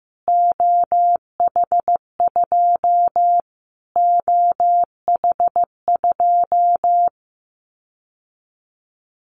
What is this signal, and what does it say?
Morse code, heard as a keyed tone, about 14 wpm, 700 Hz. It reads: OH2 OH2
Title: Callsign Prefixes for Top 25 DXCC - Repeated 3x - 15wpm